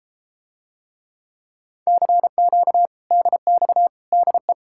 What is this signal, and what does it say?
Morse code, heard as a keyed tone, about 33 wpm, 700 Hz. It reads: CQ DX DE